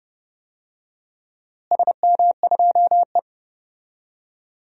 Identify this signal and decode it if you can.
Morse code, heard as a keyed tone, about 30 wpm, 700 Hz. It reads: SM2E
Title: DXCC-100 Call Signs - 4 Characters - Repeated 2x - 30wpm